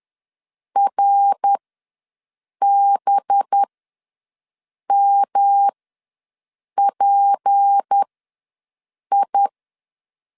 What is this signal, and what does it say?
Morse code, heard as a keyed tone, about 10 wpm, spaced as for 6 wpm, 800 Hz. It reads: RBMPI
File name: LEZIONI / 7 GF9Z / 7 GF ws 10wpm 3918